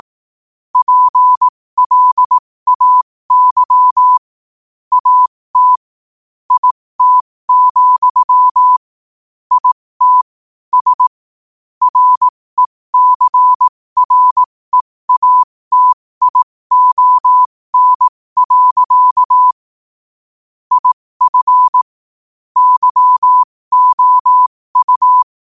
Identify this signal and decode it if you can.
Morse code, heard as a keyed tone, about 18 wpm, spaced as for 15 wpm, 1000 Hz. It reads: PLAY AT IT, IT S RECREATION. IF YOU
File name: qotdfiles / 15wpm